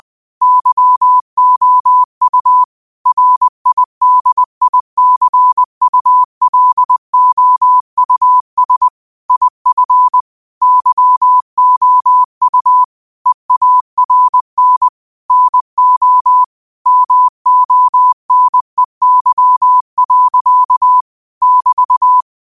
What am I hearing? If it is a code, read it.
Morse code, heard as a keyed tone, about 20 wpm, 1000 Hz. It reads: YOU RIDICULOUS IF YOU EARN NO MONEY. <BT>